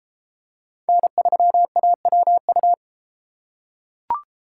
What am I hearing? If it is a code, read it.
Morse code, heard as a keyed tone, about 29 wpm, 700 Hz. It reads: N3AWU